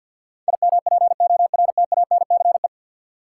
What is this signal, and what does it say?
Morse code, heard as a keyed tone, about 53 wpm, 700 Hz. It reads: IMPORTANCE
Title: Sentences from Top 1200 Words - New Word Drill Alt - 50wpm